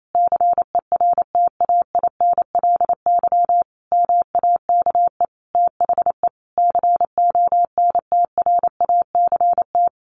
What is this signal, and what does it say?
Morse code, heard as a keyed tone, about 28 wpm, 700 Hz. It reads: CERTAINLY MAKE THE CONTRACT
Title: Sentences from Top 1400 Words 28wpm